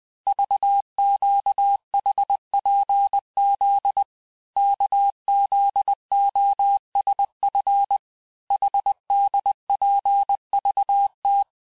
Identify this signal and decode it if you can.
Morse code, heard as a keyed tone, about 20 wpm, 800 Hz. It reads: VQHPZ KZOSF HDPVT
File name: cours-1584-teste-chiffre-vitesse-19mpm